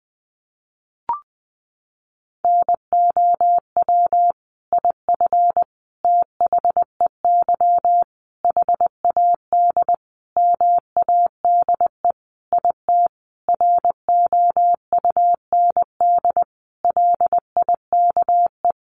Morse code, heard as a keyed tone, about 20 wpm, 700 Hz. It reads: NOW IF THEY HAD MADE IT ROUND LIKE
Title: Sentences from Top 500 Words 20wpm